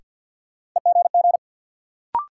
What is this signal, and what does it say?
Morse code, heard as a keyed tone, about 52 wpm, 700 Hz. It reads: EGG
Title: Sentences from Top 1000 Words - Review Part 7 - New Word Drill - 50wpm